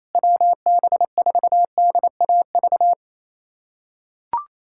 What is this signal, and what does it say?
Morse code, heard as a keyed tone, about 27 wpm, 700 Hz. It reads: WB4DAV
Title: USA Callsign DE Callsign 28wpm